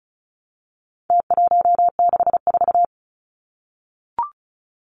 Morse code, heard as a keyed tone, about 34 wpm, 700 Hz. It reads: T164
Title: Numbers - Contest Style - 4-Digit 35wpm